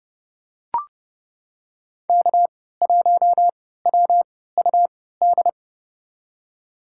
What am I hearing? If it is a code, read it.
Morse code, heard as a keyed tone, about 30 wpm, spaced as for 17 wpm, 700 Hz. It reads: K1WUD
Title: USA Call Signs - 5 Characters - 3x Character Spacing - 30wpm